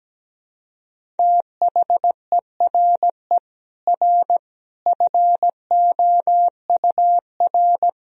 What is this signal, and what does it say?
Morse code, heard as a keyed tone, about 17 wpm, 700 Hz. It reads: THERE R FOUR
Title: Sentences from Top 400 Words and Abbreviations 17wpm